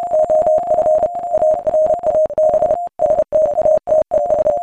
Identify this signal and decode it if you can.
Morse code, heard as a keyed tone, about 28 wpm, 620 Hz. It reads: OF ARAB STATES
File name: MorseCodeChoir